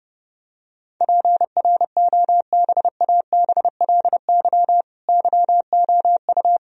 Morse code, heard as a keyed tone, about 32 wpm, 700 Hz. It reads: PROBABLY YOU
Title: Sentences from Top 1000 Words - Review Part 4 - 30wpm